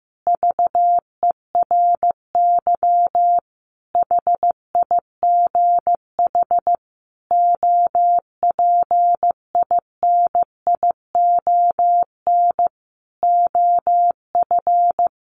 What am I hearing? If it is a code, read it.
Morse code, heard as a keyed tone, about 15 wpm, 700 Hz. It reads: VERY HIGH OPINION OF
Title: Sentences from Top 1300 Words 15wpm